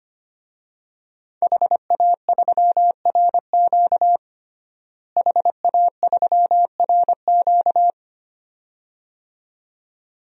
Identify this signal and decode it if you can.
Morse code, heard as a keyed tone, about 25 wpm, 700 Hz. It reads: HA3RQ HA3RQ
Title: DXCC-100 Call Signs - 5 Characters - Repeated 2x - 25wpm